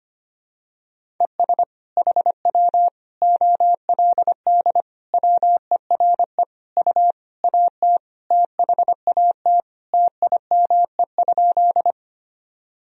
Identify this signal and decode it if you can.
Morse code, heard as a keyed tone, about 25 wpm, 700 Hz. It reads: ES HW OLD WERE U AT THAT TIME?
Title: Sentences from Top 200 Words and Abbreviations 25wpm